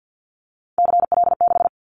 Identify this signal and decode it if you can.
Morse code, heard as a keyed tone, about 50 wpm, 700 Hz. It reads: CL6
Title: Callsign Prefixes for Top 100 DXCC - 50wpm